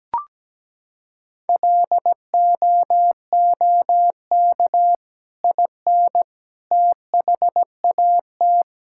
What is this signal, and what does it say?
Morse code, heard as a keyed tone, about 17 wpm, 700 Hz. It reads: LOOK IN THAT
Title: Sentences from Top 1000 Words - Review Part 5 - 17wpm